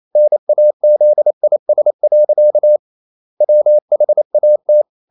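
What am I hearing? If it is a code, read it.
Morse code, heard as a keyed tone, about 28 wpm, 600 Hz. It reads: NAZIS. WHAT
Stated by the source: Morse code practice at 28 wpm, text of Ken Follett's novel Never in English, from CW med Gnister.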